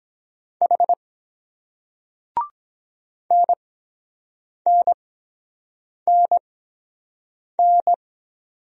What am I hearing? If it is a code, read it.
Morse code, heard as a keyed tone, about 23 wpm, spaced as for 7 wpm, 700 Hz. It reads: H NNNN